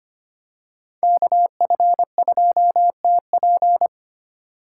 Morse code, heard as a keyed tone, about 25 wpm, 700 Hz. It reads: KF2TP
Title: K1USN SST QSO Practice - 25wpm